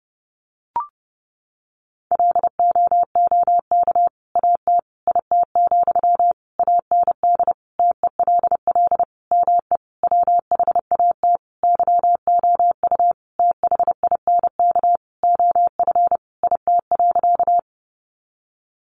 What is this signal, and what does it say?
Morse code, heard as a keyed tone, about 30 wpm, 700 Hz. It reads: LOOK AT IT, AND TELL ME WHAT YOU THINK OF IT.
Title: Sentences from Top 200 Words 30wpm